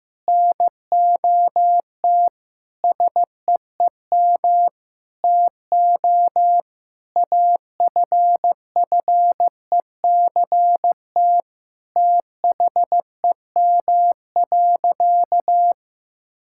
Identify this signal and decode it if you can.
Morse code, heard as a keyed tone, about 15 wpm, 700 Hz. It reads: NOT SEEM TO AFFECT THEM.
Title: Sentences from Top 1000 Words - Review Part 6 - 15wpm